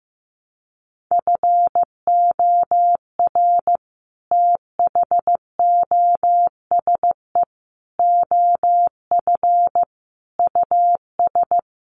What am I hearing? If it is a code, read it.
Morse code, heard as a keyed tone, about 15 wpm, 700 Hz. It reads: FOR THOSE OF US
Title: Sets of 5 Words Encore 15wpm